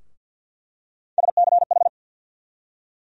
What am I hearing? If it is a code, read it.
Morse code, heard as a keyed tone, about 50 wpm, 700 Hz. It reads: SCH
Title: USA Call Sign Postfixes - 3 Letters 50wpm